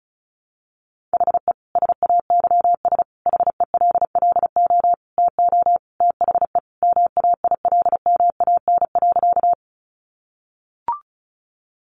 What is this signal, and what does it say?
Morse code, heard as a keyed tone, about 34 wpm, 700 Hz. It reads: HE SAYS HELLO TO THE MAILMAN.
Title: Additional Easy Phrases 35wpm